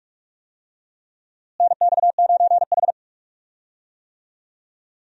Morse code, heard as a keyed tone, about 46 wpm, 700 Hz. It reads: NX9H
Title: USA Call Signs - 4 Characters - Repeated 3x - 45wpm